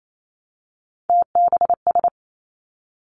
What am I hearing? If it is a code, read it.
Morse code, heard as a keyed tone, about 28 wpm, 700 Hz. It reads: TBS
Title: Three Challenging Letter-Number Combinations 28wpm